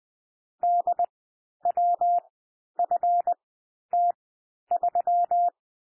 Morse code, heard as a keyed tone, about 20 wpm, spaced as for 11 wpm, 700 Hz. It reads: DWFT3
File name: cwmp3 / M1 SOUND MIXED 20 wpm